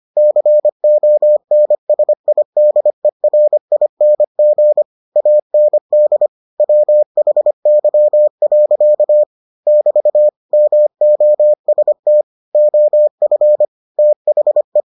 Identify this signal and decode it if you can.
Morse code, heard as a keyed tone, about 25 wpm, 600 Hz. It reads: CONSIDERING AND WHY. <BT> MOST OF THE